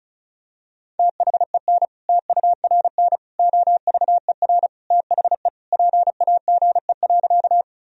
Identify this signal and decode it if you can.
Morse code, heard as a keyed tone, about 34 wpm, 700 Hz. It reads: THEN TURN OVER THE PAGE.
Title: Sentences from Top 1000 Words - Review Part 1 - 35wpm